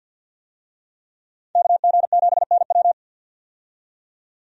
Morse code, heard as a keyed tone, about 51 wpm, 700 Hz. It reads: KG7NW
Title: USA Call Signs - 5 Characters - 50wpm